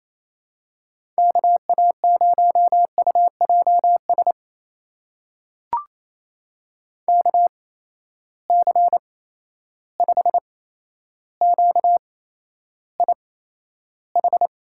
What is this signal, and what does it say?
Morse code, heard as a keyed tone, about 28 wpm, spaced as for 12 wpm, 700 Hz. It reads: KA0UJS KC5QIH
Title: USA Call Signs - 6 Characters - 8x Character Spacing - 28wpm